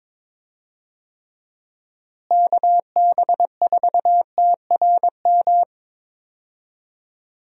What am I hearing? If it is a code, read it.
Morse code, heard as a keyed tone, about 22 wpm, 700 Hz. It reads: KB4TRM
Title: USA Call Signs - 6 Characters - Repeated 2x - 22wpm